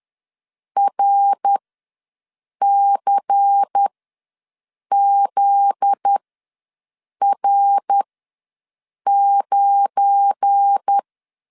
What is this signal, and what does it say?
Morse code, heard as a keyed tone, about 10 wpm, spaced as for 6 wpm, 800 Hz. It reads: RCZR9
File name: LEZIONI / 8 YJ45 / 8 YJ ws 10wpm 4043